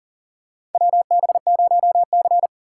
Morse code, heard as a keyed tone, about 39 wpm, 700 Hz. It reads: WB0C
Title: USA Call Signs - 4 Characters - 2x Character Spacing - 40wpm